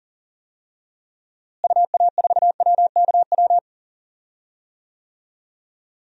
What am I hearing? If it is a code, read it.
Morse code, heard as a keyed tone, about 40 wpm, 700 Hz. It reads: UA4WKW
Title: DXCC-100 Call Signs - 6 Characters - Repeated 3x - 40wpm